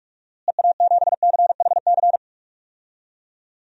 Morse code, heard as a keyed tone, about 44 wpm, 700 Hz. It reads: EA7CHC
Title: DXCC-100 Call Signs - 6 Characters - Repeated 3x - 45wpm